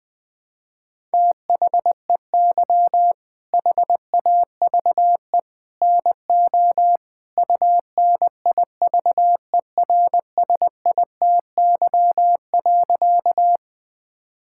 Morse code, heard as a keyed tone, about 20 wpm, 700 Hz. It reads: THEY HAVE NO UNIVERSITY.
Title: Sentences from Top 1000 Words - Review Part 5 - 20wpm